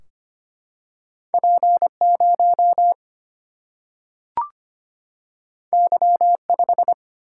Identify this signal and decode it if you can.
Morse code, heard as a keyed tone, about 25 wpm, 700 Hz. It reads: P 0 Y 5
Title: Single-Letter Single-Number 25wpm